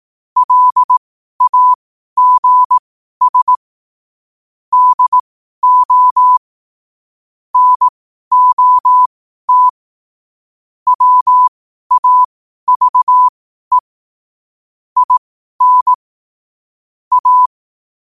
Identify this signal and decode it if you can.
Morse code, heard as a keyed tone, about 18 wpm, spaced as for 12 wpm, 1000 Hz. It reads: LAGS DO NOT WAVE IN A
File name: qotdfiles / 12wpm